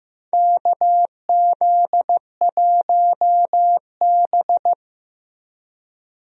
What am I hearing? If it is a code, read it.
Morse code, heard as a keyed tone, about 16 wpm, 700 Hz. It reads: KZ1B
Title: USA Call Signs - 4 Characters - Repeated 3x - 15wpm